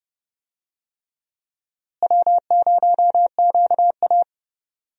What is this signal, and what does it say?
Morse code, heard as a keyed tone, about 30 wpm, 700 Hz. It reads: W0QA
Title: USA Call Signs - 4 Characters - Repeated 3x - 30wpm